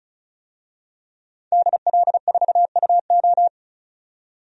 Morse code, heard as a keyed tone, about 36 wpm, 700 Hz. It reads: DL4UO
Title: DXCC-100 Call Signs - 5 Characters - 35wpm